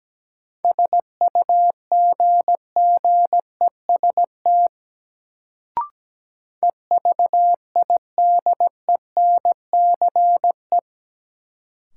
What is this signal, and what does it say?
Morse code, heard as a keyed tone, about 17 wpm, 700 Hz. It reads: SUGGEST EVIDENCE
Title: Sentences from Top 1300 Words - New Word Drill - 17wpm